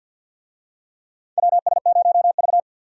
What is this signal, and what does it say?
Morse code, heard as a keyed tone, about 53 wpm, 700 Hz. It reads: WS0V
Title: USA Call Signs - 4 Characters - 50wpm